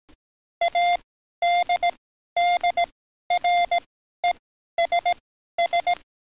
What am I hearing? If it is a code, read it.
Morse code, heard as a keyed tone, about 18 wpm, spaced as for 12 wpm, 700 Hz. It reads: ADDRESS
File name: training / NAVPERS 11049to11060 / NAVPERS 11053 RA RB